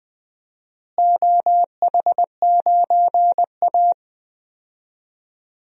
Morse code, heard as a keyed tone, about 20 wpm, 700 Hz. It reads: OH9A
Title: DXCC-100 Call Signs - 4 Characters - Repeated 2x - 20wpm